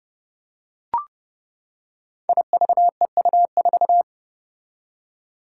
Morse code, heard as a keyed tone, about 30 wpm, 700 Hz. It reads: IVEU4
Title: Five Challenging Letter-Number Combinations 30wpm